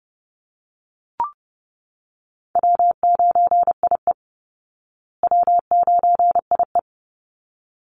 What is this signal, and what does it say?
Morse code, heard as a keyed tone, about 30 wpm, 700 Hz. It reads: W9IE W9IE
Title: Easy Call Signs Repeated 2x - 30wpm